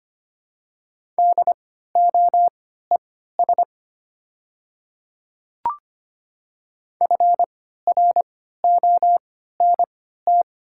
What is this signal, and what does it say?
Morse code, heard as a keyed tone, about 25 wpm, spaced as for 14 wpm, 700 Hz. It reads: DOES FRONT
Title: 3x Character Spacing - Top 500 Words - 25wpm